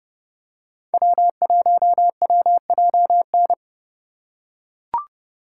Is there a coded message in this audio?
Yes: Morse code from 0.9 to 3.5 s, heard as a keyed tone, about 30 wpm, 700 Hz. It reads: W1WJN